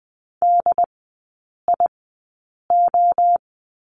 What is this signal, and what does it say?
Morse code, heard as a keyed tone, about 20 wpm, spaced as for 8 wpm, 700 Hz. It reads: DIO
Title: 074 Lesson 13e - Letter L - Three Characters 20wpm